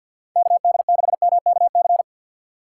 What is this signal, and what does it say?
Morse code, heard as a keyed tone, about 52 wpm, 700 Hz. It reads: KD6MKC
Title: USA Call Signs - 6 Characters - 3x Character Spacing - 50wpm